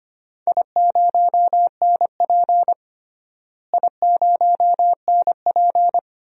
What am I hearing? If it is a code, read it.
Morse code, heard as a keyed tone, about 25 wpm, 700 Hz. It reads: I0NP I0NP
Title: DXCC-100 Call Signs - 4 Characters - Repeated 3x - 25wpm